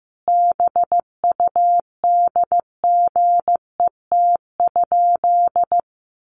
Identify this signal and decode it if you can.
Morse code, heard as a keyed tone, about 15 wpm, 700 Hz. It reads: BUDGET?
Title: Sentences from Top 1200 Words 15wpm